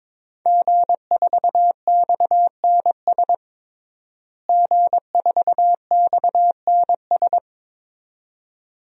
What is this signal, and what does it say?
Morse code, heard as a keyed tone, about 22 wpm, 700 Hz. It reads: G4XNS G4XNS
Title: DXCC-100 Call Signs - 5 Characters - Repeated 2x - 22wpm